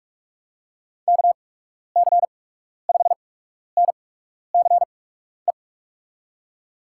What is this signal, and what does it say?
Morse code, heard as a keyed tone, about 48 wpm, spaced as for 12 wpm, 700 Hz. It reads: KC5NCE
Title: USA Call Signs - 6 Characters - 8x Character Spacing - 45wpm